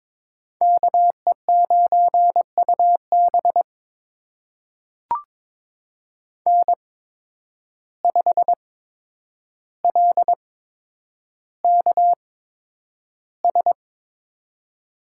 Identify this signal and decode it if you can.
Morse code, heard as a keyed tone, about 22 wpm, spaced as for 9 wpm, 700 Hz. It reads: KE9UB N5LKS
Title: USA Call Signs - 5 Characters - 8x Character Spacing - 22wpm